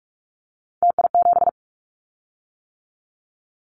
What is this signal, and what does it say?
Morse code, heard as a keyed tone, about 46 wpm, 700 Hz. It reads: TI7
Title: Callsign Prefixes for Top 100 DXCC - 45wpm